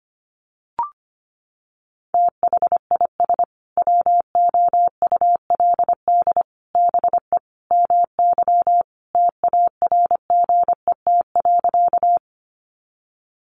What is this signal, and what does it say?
Morse code, heard as a keyed tone, about 25 wpm, 700 Hz. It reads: THIS WOULD BE MY TARGET.
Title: Sentences from Top 1200 Words 25wpm